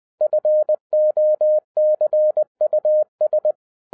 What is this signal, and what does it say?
Morse code, heard as a keyed tone, about 20 wpm, 610 Hz. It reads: FOCUS
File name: 2011-10-mp3 / n13-TIPS FROM MY 34 YEARS OF EXPERIENCE 20wpm